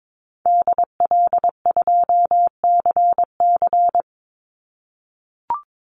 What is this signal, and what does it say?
Morse code, heard as a keyed tone, about 22 wpm, 700 Hz. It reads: DL2CC
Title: Call Signs - 22wpm